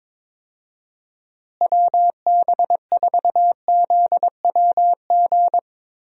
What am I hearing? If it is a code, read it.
Morse code, heard as a keyed tone, about 22 wpm, 700 Hz. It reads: WB4ZWG
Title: USA Call Signs - 6 Characters - Repeated 3x - 22wpm